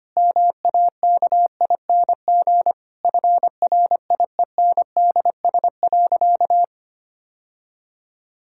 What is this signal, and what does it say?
Morse code, heard as a keyed tone, about 25 wpm, 700 Hz. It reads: MAKING FRIENDS.
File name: practice / 25wpm / Difficult Sentences with Hints - 25wpm